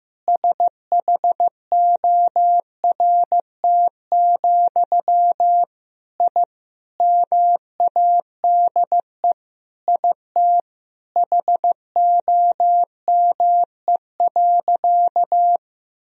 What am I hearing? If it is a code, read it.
Morse code, heard as a keyed tone, about 16 wpm, 700 Hz. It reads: SHORT, I MADE IT HOME.
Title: Sentences from Top 1000 Words - Review Part 2 - 15wpm